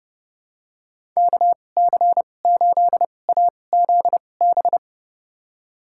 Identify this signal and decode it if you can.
Morse code, heard as a keyed tone, about 32 wpm, spaced as for 22 wpm, 700 Hz. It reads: KC8AZB
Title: USA Call Signs - 6 Characters - 2x Character Spacing - 30wpm